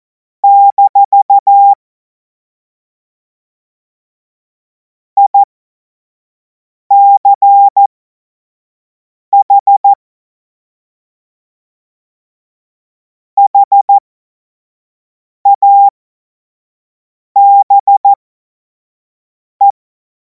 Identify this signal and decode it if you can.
Morse code, heard as a keyed tone, about 14 wpm, spaced as for 5 wpm, 800 Hz. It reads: - ICH HABE
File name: Morsekurs / Witze / Witz 31. März 2026 25 F70